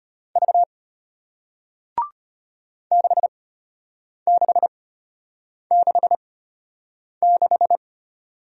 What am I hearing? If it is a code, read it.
Morse code, heard as a keyed tone, about 34 wpm, spaced as for 8 wpm, 700 Hz. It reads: V 6666